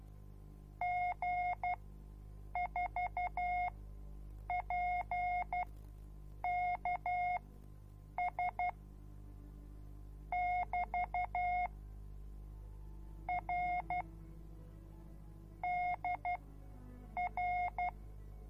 Morse code, heard as a keyed tone, about 12 wpm, spaced as for 7 wpm, 720 Hz. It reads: G4PKS <BT> R DR